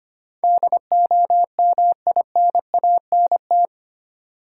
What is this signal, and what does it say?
Morse code, heard as a keyed tone, about 25 wpm, 700 Hz. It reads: DOMINANT
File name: 8 Letter Words - First 3 Letters Given - 25wpm